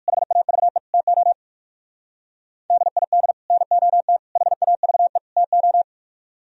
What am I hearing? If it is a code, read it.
Morse code, heard as a keyed tone, about 55 wpm, 700 Hz. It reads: HAVE TO DID NOT HAVE TO
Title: Speed-Racing - Sets of 4 Words 45wpm